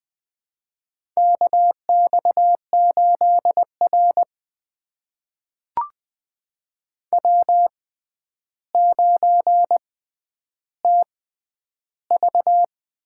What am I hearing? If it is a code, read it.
Morse code, heard as a keyed tone, about 20 wpm, spaced as for 10 wpm, 700 Hz. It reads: KX8R W9TV
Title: USA Call Signs - 4 Characters - 6x Character Spacing - 20wpm